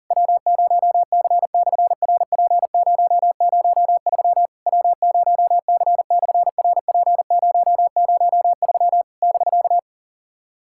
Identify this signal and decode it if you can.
Morse code, heard as a keyed tone, about 39 wpm, 700 Hz. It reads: W0C/RP003 W0C/RP003 <BK>
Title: SOTA QSO Practice 40wpm